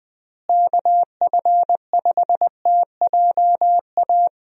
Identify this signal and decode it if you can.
Morse code, heard as a keyed tone, about 20 wpm, 700 Hz. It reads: KF5TJA
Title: USA Call Signs - 6 Characters - Repeated 2x - 20wpm